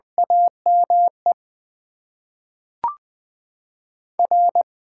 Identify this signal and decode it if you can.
Morse code, heard as a keyed tone, about 20 wpm, 700 Hz. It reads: AME R
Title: Top 20 QSO Elements - 8x Character Spacing - 20wpm